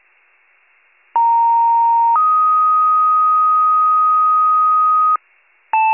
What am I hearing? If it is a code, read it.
Two-tone page, heard as a one-second tone, then a three-second tone, about 930 then 1250 Hz. It is heard from 1.1 to 5.2 s.